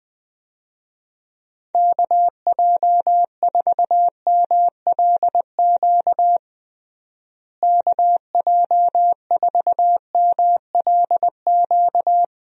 Morse code, heard as a keyed tone, about 20 wpm, 700 Hz. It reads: KJ4MLQ KJ4MLQ
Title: USA Call Signs - 6 Characters - Repeated 3x - 20wpm